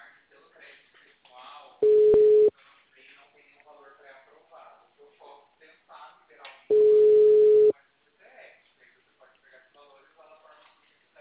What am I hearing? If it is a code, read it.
Ringback tone (Europe, 425 Hz), with speech.
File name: out- -2002- - - .69031